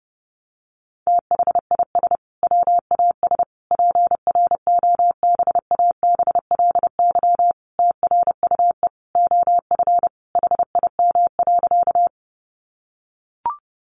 Morse code, heard as a keyed tone, about 30 wpm, 700 Hz. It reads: THIS WAS PROBABLY TRUE OF HIM.
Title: Sentences from Top 700 Words 30wpm